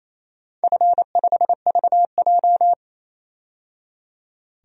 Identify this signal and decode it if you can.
Morse code, heard as a keyed tone, about 28 wpm, 700 Hz. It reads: F5VJ
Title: DXCC-100 Call Signs - 4 Characters - 28wpm